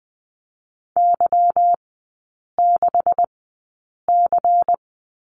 Morse code, heard as a keyed tone, about 20 wpm, spaced as for 8 wpm, 690 Hz. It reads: Y6C
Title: 159 Lesson 26e - Letter g - Three Characters 20wpm